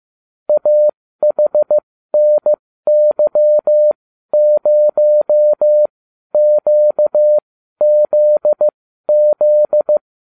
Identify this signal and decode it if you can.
Morse code, heard as a keyed tone, about 15 wpm, spaced as for 12 wpm, 600 Hz. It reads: AHNY0QZZ